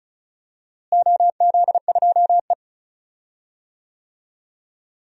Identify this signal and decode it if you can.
Morse code, heard as a keyed tone, about 36 wpm, 700 Hz. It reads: OZ2E
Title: DXCC-100 Call Signs - 4 Characters - Repeated 3x - 35wpm